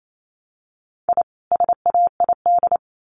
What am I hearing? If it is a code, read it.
Morse code, heard as a keyed tone, about 28 wpm, 700 Hz. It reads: I SAID
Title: Sets of 2 Words 28wpm